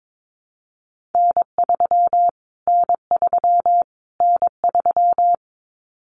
Morse code, heard as a keyed tone, about 22 wpm, 700 Hz. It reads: N3 N3 N3
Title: Callsign Prefixes for Top 25 DXCC - Repeated 3x - 22wpm